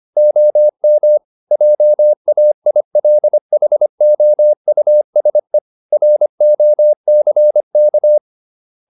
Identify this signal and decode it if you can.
Morse code, heard as a keyed tone, about 25 wpm, 600 Hz. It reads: OM JAILHOUSE ROCK
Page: Ferie 25 wpm